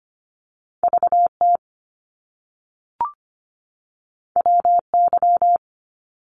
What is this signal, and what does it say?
Morse code, heard as a keyed tone, about 25 wpm, 700 Hz. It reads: VT WY
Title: US States 25wpm